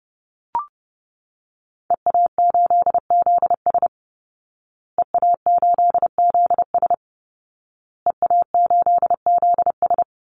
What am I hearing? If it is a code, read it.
Morse code, heard as a keyed tone, about 28 wpm, 700 Hz. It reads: EA8ZS EA8ZS EA8ZS
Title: Call Signs Repeated 2x - 30wpm